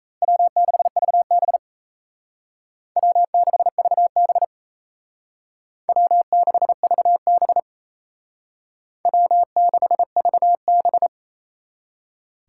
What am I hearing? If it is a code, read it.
Morse code, heard as a keyed tone, about 36 wpm, 700 Hz. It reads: W6VB W6VB W6VB W6VB